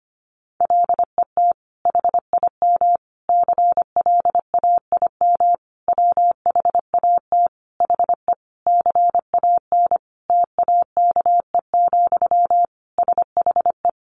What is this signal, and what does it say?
Morse code, heard as a keyed tone, about 25 wpm, 700 Hz. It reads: LET HIM CLAIM WHAT HE CAN TAKE, SHE
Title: Sentences from Top 1500 Words 25wpm